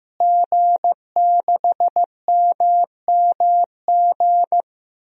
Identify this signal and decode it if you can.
Morse code, heard as a keyed tone, about 15 wpm, 700 Hz. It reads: G6MMG
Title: DXCC-100 Call Signs - 5 Characters - Repeated 3x - 15wpm